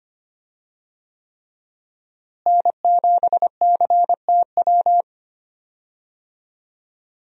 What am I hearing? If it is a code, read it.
Morse code, heard as a keyed tone, about 25 wpm, 700 Hz. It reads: N7CTW
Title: USA Call Signs - 5 Characters - Repeated 3x - 25wpm